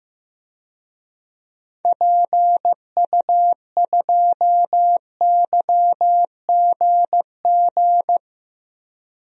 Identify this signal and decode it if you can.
Morse code, heard as a keyed tone, about 15 wpm, 700 Hz. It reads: PU2YGG
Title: DXCC-100 Call Signs - 6 Characters - Repeated 3x - 15wpm